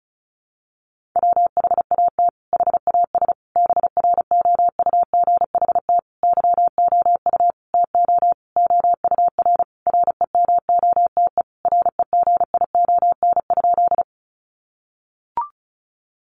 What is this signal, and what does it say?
Morse code, heard as a keyed tone, about 34 wpm, 700 Hz. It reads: WHAT HAS BROUGHT YOU TO OUR REMOTE REGION?
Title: Sentences from Top 1500 Words 35wpm